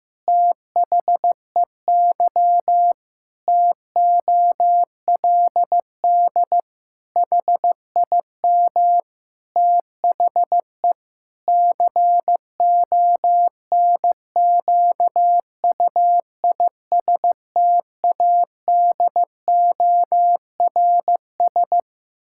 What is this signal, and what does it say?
Morse code, heard as a keyed tone, about 15 wpm, 700 Hz. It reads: THEY TOLD HIM THE CONQUISTADORS
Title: Pangram Challenge v2 - 15wpm